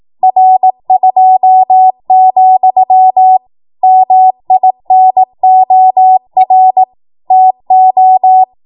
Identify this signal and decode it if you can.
Morse code, heard as a keyed tone, about 18 wpm, 750 Hz. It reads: R2, MINOR TO
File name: Morse / 18WPM